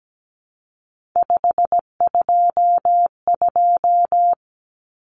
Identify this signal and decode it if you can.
Morse code, heard as a keyed tone, about 17 wpm, 700 Hz. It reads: 522
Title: Common Cut Numbers - 3-Digit 17wpm